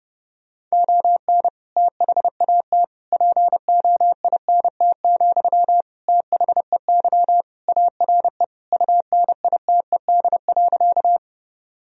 Morse code, heard as a keyed tone, about 30 wpm, 700 Hz. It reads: ON THAT POINT, THEY ARE UNITED.
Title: Sentences from Top 1000 Words - Review Part 4 - 30wpm